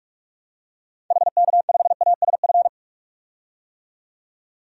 Morse code, heard as a keyed tone, about 46 wpm, 700 Hz. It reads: HK5ASF
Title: DXCC-100 Call Signs - 6 Characters - 45wpm